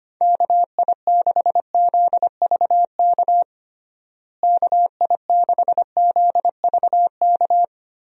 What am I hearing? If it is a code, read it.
Morse code, heard as a keyed tone, about 25 wpm, 700 Hz. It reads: KI6ZVK KI6ZVK
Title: USA Call Signs - 6 Characters - Repeated 3x - 25wpm